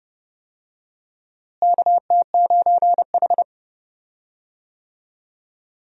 Morse code, heard as a keyed tone, about 30 wpm, 700 Hz. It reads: KT9H
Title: USA Call Signs - 4 Characters - Repeated 3x - 30wpm